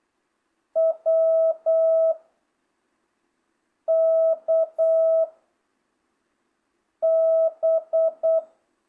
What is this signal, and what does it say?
Morse code, heard as a keyed tone, about 8 wpm, 660 Hz. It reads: W K B